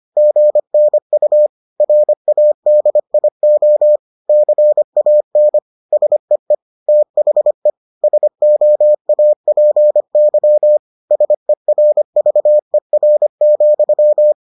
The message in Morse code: GNU RADIO CAN SEE THE SOAPY SERVER,

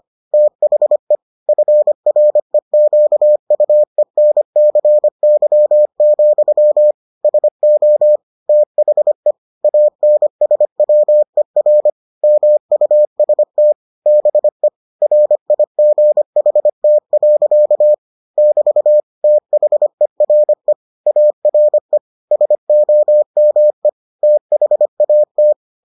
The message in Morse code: THE FREQUENCY, SO THE ANSWER MUST BE RIGHT. <BT> THERE ARE SOME THAT

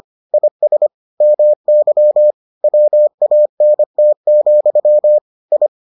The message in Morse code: IS MY WANT, I

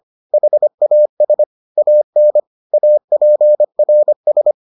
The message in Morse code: HAS AN APRS

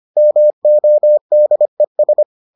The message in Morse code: MODES

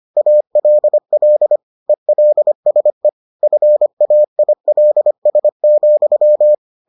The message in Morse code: ALL ELSE FAILS,

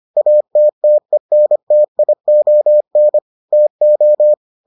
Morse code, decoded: ATTENTION TO

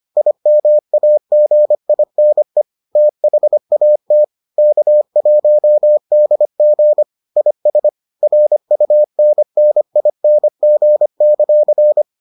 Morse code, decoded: IMAGINE THAT K1DG IS RUNNING;

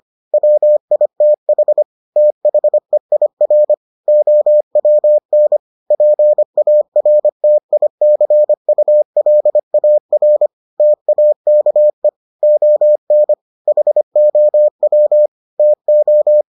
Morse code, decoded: WITH THEIR OWN PARTICULAR TAKE ON HOW TO

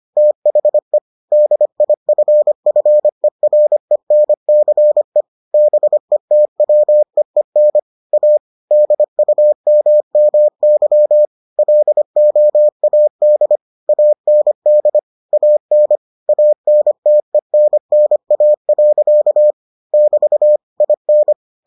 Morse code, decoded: THE DIFFERENCE BETWEEN A DUMMY LOAD AND AN ANTENNA. <BT> IN

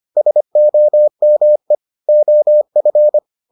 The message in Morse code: SOME OF

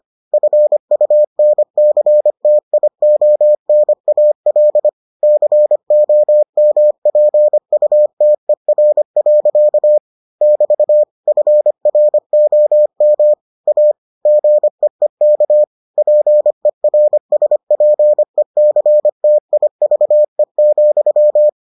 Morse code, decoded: FUNCTIONAL COMPUTER. <BT> FROM A GEEK PERSPECTIVE,

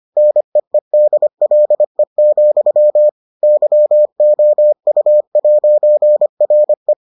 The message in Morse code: NEEDLE, YOU'RE

